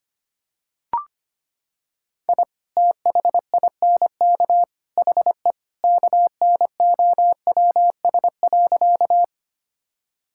Morse code, heard as a keyed tone, about 25 wpm, 700 Hz. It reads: I THINK HE KNOWS.
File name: practice / 25wpm / Sentences with 3-Word Phrases - 25wpm